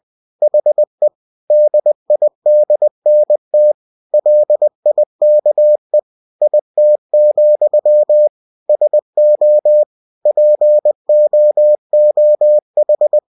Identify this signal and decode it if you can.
Morse code, heard as a keyed tone, about 20 wpm, 600 Hz. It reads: HE DIDNT LIKE IT, SO POOH
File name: cw-books / winnie-the-pooh / winnie-20wpm-page-0045